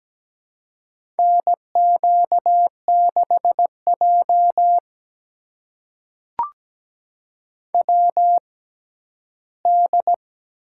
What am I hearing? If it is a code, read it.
Morse code, heard as a keyed tone, about 17 wpm, spaced as for 11 wpm, 700 Hz. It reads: NQ6J WD
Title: USA Call Signs - 4 Characters - 6x Character Spacing - 17wpm